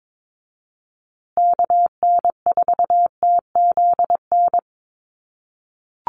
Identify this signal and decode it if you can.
Morse code, heard as a keyed tone, about 22 wpm, 700 Hz. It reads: KN4TZN
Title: USA Call Signs Repeated 2x 22wpm